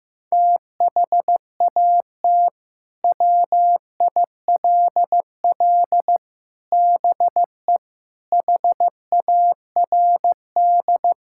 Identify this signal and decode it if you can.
Morse code, heard as a keyed tone, about 15 wpm, 700 Hz. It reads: THAT WILL BE HARD
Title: Sentences from Top 1000 Words - Review Part 5 - 15wpm